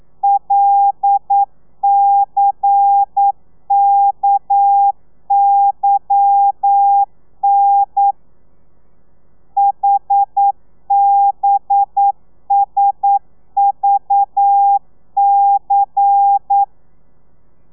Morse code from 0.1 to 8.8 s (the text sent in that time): LCKYN